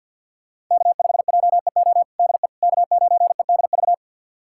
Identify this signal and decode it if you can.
Morse code, heard as a keyed tone, about 51 wpm, 700 Hz. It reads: K5JEO DE K9EDV
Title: USA Callsign DE Callsign - 5 Characters - 50wpm